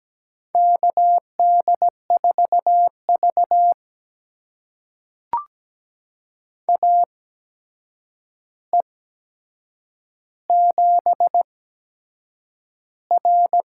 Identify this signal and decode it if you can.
Morse code, heard as a keyed tone, about 17 wpm, spaced as for 7 wpm, 700 Hz. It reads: KD4V AE7R
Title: USA Call Signs - 4 Characters - 8x Character Spacing - 17wpm